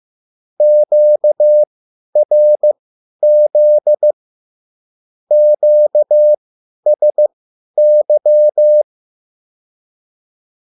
Morse code, heard as a keyed tone, about 15 wpm, spaced as for 10 wpm, 600 Hz. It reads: QRZ QSY
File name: QSL / CW lesson